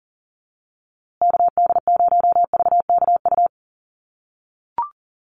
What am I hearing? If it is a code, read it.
Morse code, heard as a keyed tone, about 38 wpm, 700 Hz. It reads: KD0VKU
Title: USA Call Signs Repeated 2x 40wpm